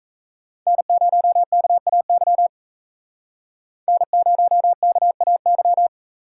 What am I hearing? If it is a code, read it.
Morse code, heard as a keyed tone, about 33 wpm, 700 Hz. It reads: N0KAY N0KAY
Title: Speed Racing - USA 1x3 Call Signs 28wpm